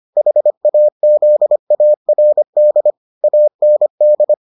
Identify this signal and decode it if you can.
Morse code, heard as a keyed tone, about 25 wpm, 600 Hz. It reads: HAZARD AND